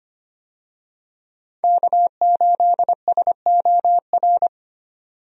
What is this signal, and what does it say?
Morse code, heard as a keyed tone, about 25 wpm, 700 Hz. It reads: K8SOR
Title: USA Call Signs - 5 Characters - 25wpm